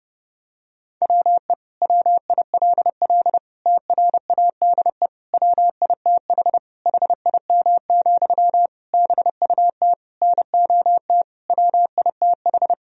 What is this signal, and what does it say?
Morse code, heard as a keyed tone, about 30 wpm, 700 Hz. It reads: WE WILL TRADE WITH HIM, BUT NOT WITH
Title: Sentences from Top 1000 Words - Review Part 4 - 30wpm